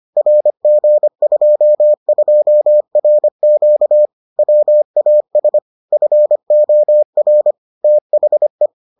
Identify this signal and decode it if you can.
Morse code, heard as a keyed tone, about 25 wpm, 600 Hz. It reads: RG22RQ WAS FOR THE